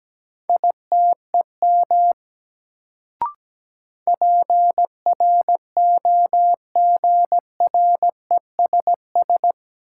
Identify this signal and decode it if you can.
Morse code, heard as a keyed tone, about 17 wpm, 700 Hz. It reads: ITEM PROGRESS
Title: Sentences from Top 1400 Words - New Word Drill Alt - 17wpm